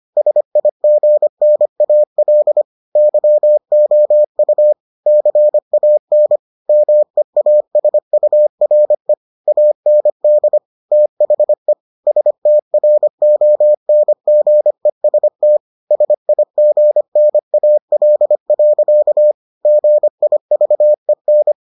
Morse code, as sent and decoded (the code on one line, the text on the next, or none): ... .. --. -. .- .-.. / -.-- --- ..- / -.-. .- -. / -- . .- ... ..- .-. / .- -. -.. / - .... / ... - .-. --- -. --. ... - / ... .. --. -. .- .-.. .-.-.- / --. .. ...- . -.
SIGNAL YOU CAN MEASURE AND THE STRONGEST SIGNAL. GIVEN